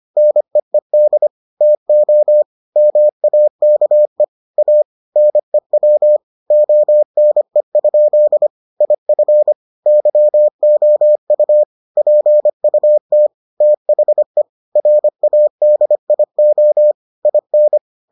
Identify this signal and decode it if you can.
Morse code, heard as a keyed tone, about 25 wpm, 600 Hz. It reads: NEED TO MAKE A NEW ONE? IF YOU PUT THE RADIO IN